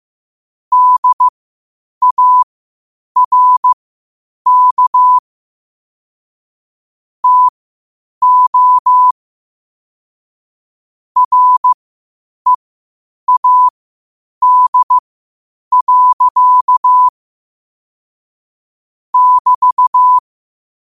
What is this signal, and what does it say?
Morse code, heard as a keyed tone, about 15 wpm, spaced as for 8 wpm, 1000 Hz. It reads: DARK TO READ. <BT>
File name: qotdfiles / 8wpm